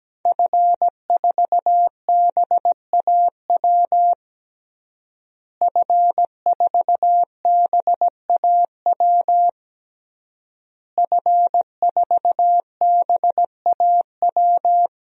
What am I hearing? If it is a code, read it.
Morse code, heard as a keyed tone, about 17 wpm, 700 Hz. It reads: F4BAW F4BAW F4BAW
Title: DXCC-100 Call Signs - 5 Characters - Repeated 3x - 17wpm